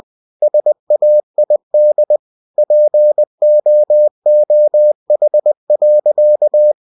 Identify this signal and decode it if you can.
Morse code, heard as a keyed tone, about 20 wpm, 600 Hz. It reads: SAID POOH.
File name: cw-books / winnie-the-pooh / winnie-20wpm-page-0084